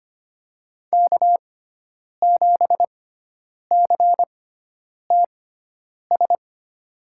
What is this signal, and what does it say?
Morse code, heard as a keyed tone, about 25 wpm, spaced as for 9 wpm, 700 Hz. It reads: K7CTS
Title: USA Call Signs - 6x Character Spacing - 25wpm